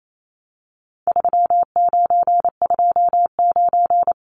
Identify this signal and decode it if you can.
Morse code, heard as a keyed tone, about 28 wpm, 700 Hz. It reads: 3929
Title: Numbers - 4-Digit 28wpm